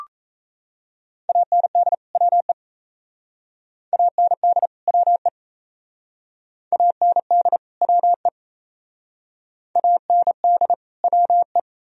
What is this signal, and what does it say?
Morse code, heard as a keyed tone, about 35 wpm, 700 Hz. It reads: AND WE AND WE AND WE AND WE
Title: Speed-Racing - Sets of 2 Words 28wpm